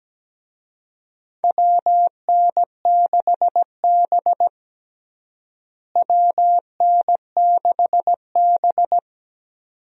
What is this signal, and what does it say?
Morse code, heard as a keyed tone, about 17 wpm, 700 Hz. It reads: WN6B WN6B
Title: USA Call Signs - 4 Characters - Repeated 2x - 17wpm